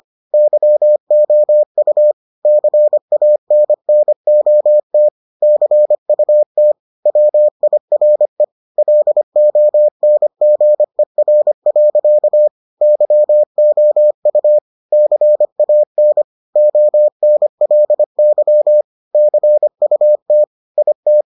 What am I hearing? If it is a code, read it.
Morse code, heard as a keyed tone, about 25 wpm, 600 Hz. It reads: YOU CANNOT CUT WIRE LONGER. YOU CAN ONLY CUT IT